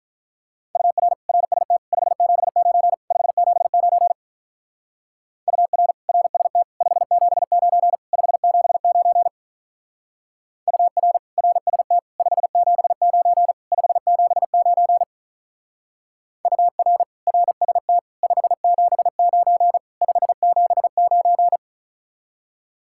Morse code, heard as a keyed tone, about 47 wpm, 700 Hz. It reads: UR RST 579 579 UR RST 579 579 UR RST 579 579 UR RST 579 579